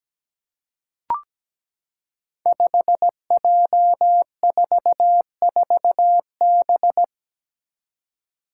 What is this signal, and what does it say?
Morse code, heard as a keyed tone, about 17 wpm, 700 Hz. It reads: 5J44B
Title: Five Challenging Letter-Number Combinations 17wpm